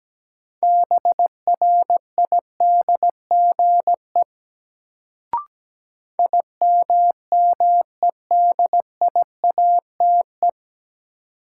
Morse code, heard as a keyed tone, about 17 wpm, 700 Hz. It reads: BRIDGE IMMEDIATE
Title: Sentences from Top 1200 Words - New Word Drill Alt - 17wpm